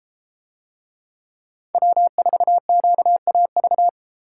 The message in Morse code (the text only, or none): W4QAV